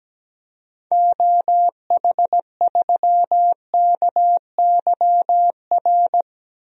Morse code, heard as a keyed tone, about 17 wpm, 700 Hz. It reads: OH3KYR
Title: DXCC-100 Call Signs - 6 Characters - 17wpm